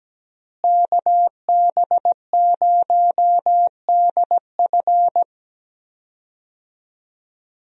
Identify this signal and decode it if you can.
Morse code, heard as a keyed tone, about 17 wpm, 700 Hz. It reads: KB0DF
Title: USA Call Signs - 5 Characters - Repeated 2x - 17wpm